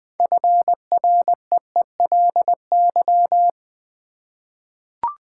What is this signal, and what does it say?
Morse code, heard as a keyed tone, about 20 wpm, 700 Hz. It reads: FREELY
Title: 6 Letter Words - First 3 Letters Given - 20wpm